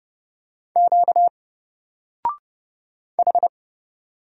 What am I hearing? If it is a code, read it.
Morse code, heard as a keyed tone, about 30 wpm, 700 Hz. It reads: Q H
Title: Single Letter-Number Alt 30wpm